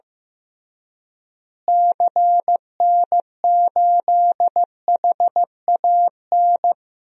Morse code, heard as a keyed tone, about 16 wpm, 700 Hz. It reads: CN8HAN
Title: DXCC-100 Call Signs - 6 Characters - Repeated 2x - 15wpm